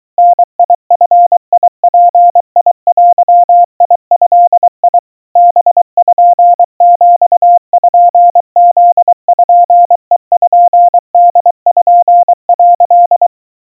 Morse code (Japanese Wabun code), heard as a keyed tone, about 24 wpm, 700 Hz. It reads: ダヂヅデド パピプペポ」
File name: kawa / Morse sample / aiueo 80 cpm 700 Hz Crispy